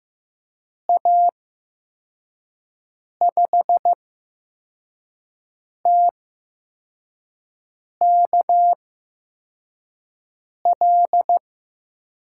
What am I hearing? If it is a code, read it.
Morse code, heard as a keyed tone, about 15 wpm, spaced as for 4 wpm, 700 Hz. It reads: A5TKL
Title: USA Call Signs - 6 Characters - 8x Character Spacing - 15wpm